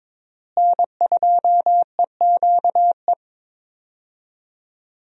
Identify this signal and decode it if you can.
Morse code, heard as a keyed tone, about 22 wpm, 700 Hz. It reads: N2EQE
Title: USA Call Signs - 5 Characters - Repeated 2x - 22wpm